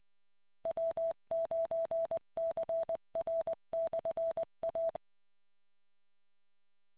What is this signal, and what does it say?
Morse code, heard as a keyed tone, about 20 wpm, 660 Hz. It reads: W9CR/A